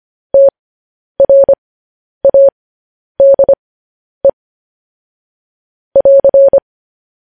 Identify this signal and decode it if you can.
Morse code, heard as a keyed tone, about 25 wpm, spaced as for 10 wpm, 550 Hz. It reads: TRADE <AR>